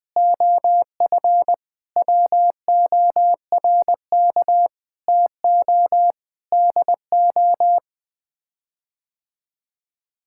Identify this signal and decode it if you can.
Morse code, heard as a keyed tone, about 20 wpm, 700 Hz. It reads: OF WORK TO DO
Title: Sets of 8 Words - 20wpm